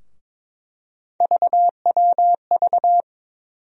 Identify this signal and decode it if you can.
Morse code, heard as a keyed tone, about 21 wpm, 700 Hz. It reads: VWV
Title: USA Call Sign Postfixes - 3 Letters 22wpm